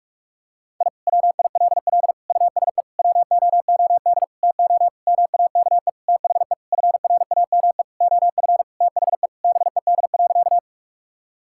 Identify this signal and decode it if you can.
Morse code, heard as a keyed tone, about 46 wpm, 700 Hz. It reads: I WILL USE WOOD TO MAKE THE FRAME OF THE BED.
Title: Sentences from Top 1000 Words - Review Part 6 - 45wpm